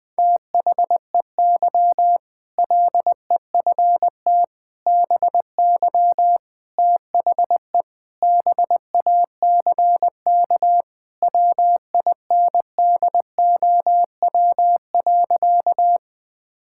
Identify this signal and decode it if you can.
Morse code, heard as a keyed tone, about 20 wpm, 700 Hz. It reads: THEY LEFT BY THE BACK WINDOW.
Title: Sentences from Top 1600 Words 20wpm